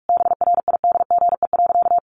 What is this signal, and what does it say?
Morse code, heard as a keyed tone, about 44 wpm, 700 Hz. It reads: BRIDGE.
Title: Sentences from Top 1200 Words 45wpm